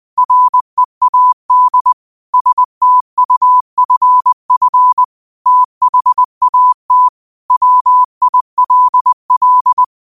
Morse code, heard as a keyed tone, about 20 wpm, 1000 Hz. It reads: READ STUFF THAT WILL